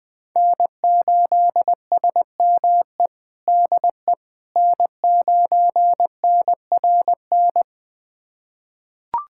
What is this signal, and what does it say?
Morse code, heard as a keyed tone, about 20 wpm, 700 Hz. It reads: N8SME DE N9NRN
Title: USA Callsign DE Callsign - 5 Characters - 20wpm